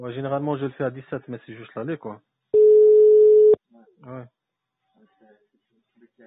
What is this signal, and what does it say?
Ringback tone (Europe, 425 Hz), with speech.